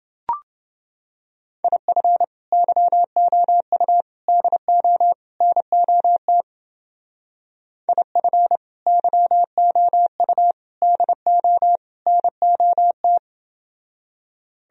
Morse code, heard as a keyed tone, about 28 wpm, 700 Hz. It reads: IF YOU DO NOT IF YOU DO NOT